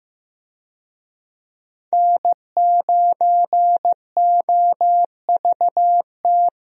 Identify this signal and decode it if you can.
Morse code, heard as a keyed tone, about 15 wpm, 700 Hz. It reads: N9OVT
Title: USA Call Signs - 5 Characters - 15wpm